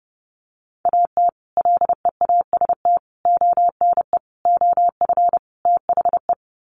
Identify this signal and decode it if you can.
Morse code, heard as a keyed tone, about 30 wpm, 700 Hz. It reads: AT LEAST ONE OF THE
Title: Sets of 5 Words Encore 30wpm